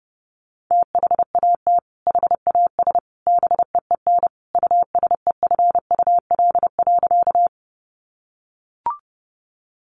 Morse code, heard as a keyed tone, about 30 wpm, 700 Hz. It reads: THAT HAS BEEN USEFUL.
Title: Sentences from Top 700 Words 30wpm